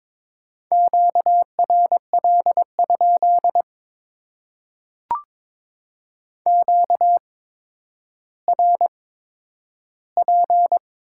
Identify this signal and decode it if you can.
Morse code, heard as a keyed tone, about 22 wpm, spaced as for 11 wpm, 700 Hz. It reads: QRL? QRP
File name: Top 90 QSO Elements - 8x Character Spacing - 22wpm